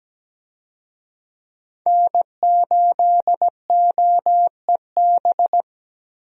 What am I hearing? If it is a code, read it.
Morse code, heard as a keyed tone, about 17 wpm, 700 Hz. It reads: N8OEB
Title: USA Call Signs - 5 Characters - 17wpm